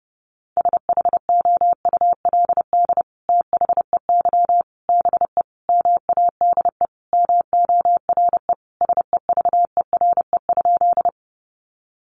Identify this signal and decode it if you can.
Morse code, heard as a keyed tone, about 28 wpm, 700 Hz. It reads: SHOULD THEY BE MADE MORE SEVERE?
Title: Sentences from Top 1400 Words 30wpm